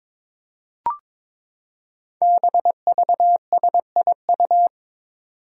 Morse code, heard as a keyed tone, about 22 wpm, 700 Hz. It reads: BVSIU
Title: Five Challenging Letter-Number Combinations 22wpm